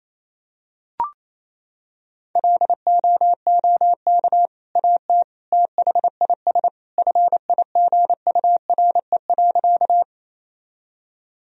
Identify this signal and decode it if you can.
Morse code, heard as a keyed tone, about 27 wpm, 700 Hz. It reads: LOOK AT THIS FIGURE.
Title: Sentences from Top 1000 Words - Review Part 2 - 28wpm